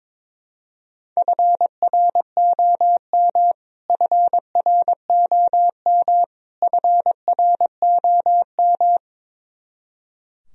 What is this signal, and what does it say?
Morse code, heard as a keyed tone, about 22 wpm, 700 Hz. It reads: FROM FROM FROM
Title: Top 100 Words Repeated 22wpm